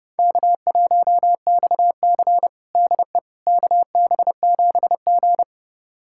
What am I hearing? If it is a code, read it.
Morse code, heard as a keyed tone, about 32 wpm, 700 Hz. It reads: K1XC DE KB7G
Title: USA Callsign DE Callsign - 4 Characters - 30wpm